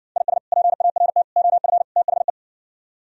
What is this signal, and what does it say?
Morse code, heard as a keyed tone, about 69 wpm, 700 Hz. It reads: IS PART OF THE